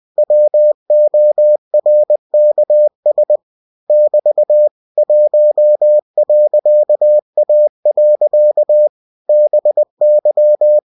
Morse code, heard as a keyed tone, about 20 wpm, 600 Hz. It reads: WORKS <BT> 1.A. BY
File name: cw-books / winnie-the-pooh / winnie-20wpm-page-0113